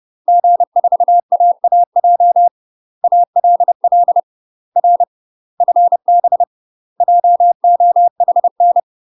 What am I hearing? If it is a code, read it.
Morse code, heard as a keyed tone, about 30 wpm, 700 Hz. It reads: G4AAJ ALL R FB JOHN
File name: UK MASTER QSOs TRAINING / 30WPM / UK MASTER QSOs 0009